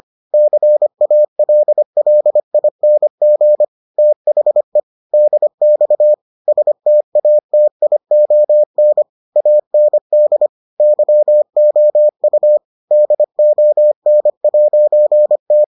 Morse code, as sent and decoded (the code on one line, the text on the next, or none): -.-. .- .-.. .-.. .. -. --. / - .... / -.. -..- / ... - .- - .. --- -. / .- -. -.. / -.-- --- ..- / -.. --- -. .----. -
CALLING THE DX STATION AND YOU DON'T